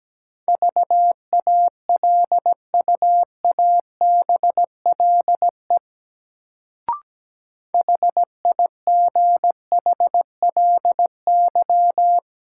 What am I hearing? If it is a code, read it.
Morse code, heard as a keyed tone, about 17 wpm, 700 Hz. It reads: VALUABLE HIGHLY
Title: Sentences from Top 1100 Words - New Word Drill - 17wpm